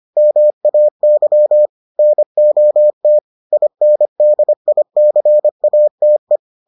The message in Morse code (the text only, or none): MAY NOT INDICATE